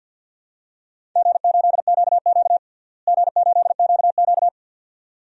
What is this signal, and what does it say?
Morse code, heard as a keyed tone, about 51 wpm, 700 Hz. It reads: G8QQ G8QQ
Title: DXCC-100 Call Signs - 4 Characters - Repeated 2x - 50wpm